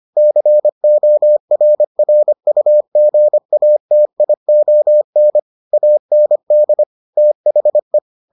Morse code, heard as a keyed tone, about 25 wpm, 600 Hz. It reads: CORRUGATION AND THE